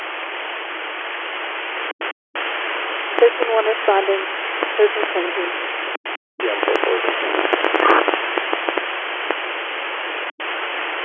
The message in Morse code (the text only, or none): none